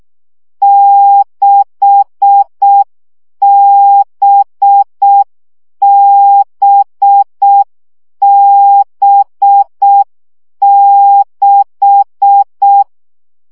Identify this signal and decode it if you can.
Morse code, heard as a keyed tone, about 6 wpm, 800 Hz. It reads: 6BBB6
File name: CW gyakorlas b 6 egyenlo minusz 5 baud